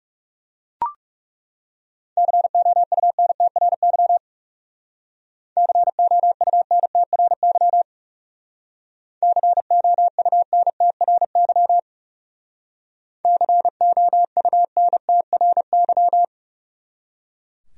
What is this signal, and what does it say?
Morse code, heard as a keyed tone, about 37 wpm, 700 Hz. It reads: COUNTRY COUNTRY COUNTRY COUNTRY
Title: Speed-Racing Top 500 Words 30wpm